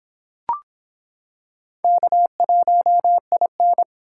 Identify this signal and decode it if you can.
Morse code, heard as a keyed tone, about 25 wpm, 700 Hz. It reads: K1IN